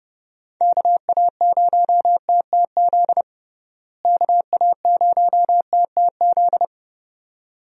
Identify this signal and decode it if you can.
Morse code, heard as a keyed tone, about 30 wpm, 700 Hz. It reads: KA0TTZ KA0TTZ
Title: USA Call Signs - 6 Characters - Repeated 2x - 30wpm